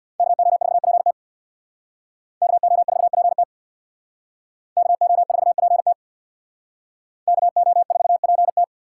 Morse code, heard as a keyed tone, about 62 wpm, 700 Hz. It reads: KO4PT KO4PT KO4PT KO4PT
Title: Speed Racing - USA Call Signs 50wpm